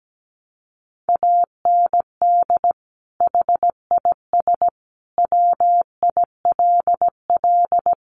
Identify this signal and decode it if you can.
Morse code, heard as a keyed tone, about 17 wpm, 700 Hz. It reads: AND HIS WILL
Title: Sentences from Top 1500 Words 17wpm